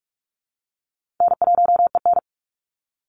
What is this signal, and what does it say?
Morse code, heard as a keyed tone, about 44 wpm, 700 Hz. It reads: N1EN
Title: Easy Call Signs Repeated 2x - 45wpm